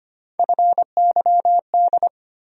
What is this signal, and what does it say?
Morse code, heard as a keyed tone, about 25 wpm, 700 Hz. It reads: FYD
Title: USA Call Sign Postfixes - 3 Letters 25wpm